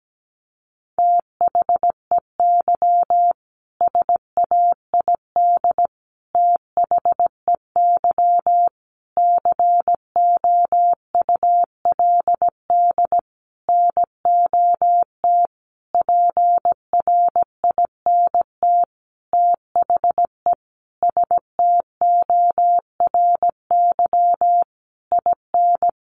Morse code, heard as a keyed tone, about 17 wpm, 700 Hz. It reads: THEY SAID THEY COULD NOT PRINT THE STORY IN
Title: Sentences from Top 1400 Words 17wpm